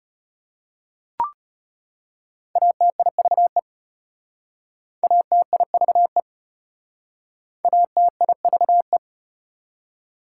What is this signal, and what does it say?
Morse code, heard as a keyed tone, about 33 wpm, 690 Hz. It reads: ATIVE ATIVE ATIVE